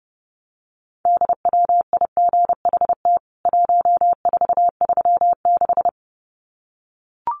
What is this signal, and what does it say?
Morse code, heard as a keyed tone, about 30 wpm, 700 Hz. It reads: DWIGHT 1436
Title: CWT Exchange 30wpm